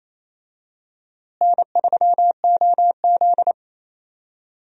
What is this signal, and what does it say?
Morse code, heard as a keyed tone, about 28 wpm, 700 Hz. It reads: N3OZ
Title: USA Call Signs - 4 Characters - Repeated 2x - 28wpm